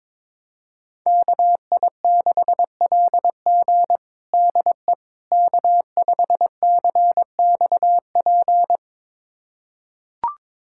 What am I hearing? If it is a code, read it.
Morse code, heard as a keyed tone, about 22 wpm, 700 Hz. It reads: KI6LG DE K5CXP
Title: USA Callsign DE Callsign - 5 Characters - 22wpm